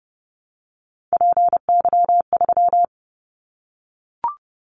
Morse code, heard as a keyed tone, about 30 wpm, 700 Hz. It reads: PY3
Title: Callsign Prefixes for Top 50 DXCC - 30wpm